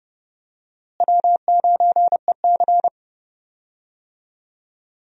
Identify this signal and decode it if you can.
Morse code, heard as a keyed tone, about 30 wpm, 700 Hz. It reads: W9EC
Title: USA Call Signs - 4 Characters - Repeated 2x - 30wpm